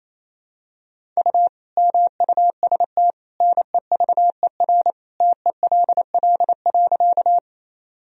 Morse code, heard as a keyed tone, about 28 wpm, 700 Hz. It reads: U MUST NEVER TELL.
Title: Sentences from Top 300 Words and Abbreviations 28wpm